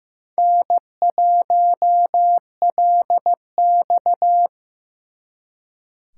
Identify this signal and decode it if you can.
Morse code, heard as a keyed tone, about 18 wpm, 700 Hz. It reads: N1LX
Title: Speed Racing - USA 1x2 Call Signs 15wpm